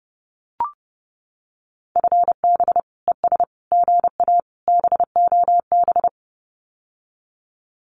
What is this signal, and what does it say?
Morse code, heard as a keyed tone, about 30 wpm, 700 Hz. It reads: FB ES GA BOB
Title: Common QSO Phrases Encore 30wpm